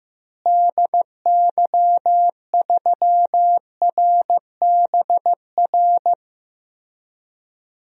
Morse code, heard as a keyed tone, about 15 wpm, 700 Hz. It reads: DY3RBR
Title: DXCC-100 Call Signs - 6 Characters - Repeated 2x - 15wpm